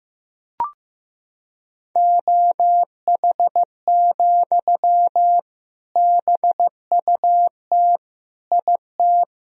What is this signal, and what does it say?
Morse code, heard as a keyed tone, about 15 wpm, 700 Hz. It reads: OH, BUT IT
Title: Sentences from Top 1000 Words - Review Part 4 - 15wpm